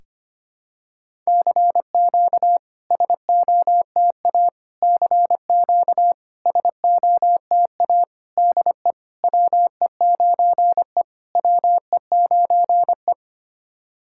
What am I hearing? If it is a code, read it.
Morse code, heard as a keyed tone, about 25 wpm, 700 Hz. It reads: CQ SOTA CQ SOTA DE WE9E WE9E
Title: SOTA QSO Practice 25wpm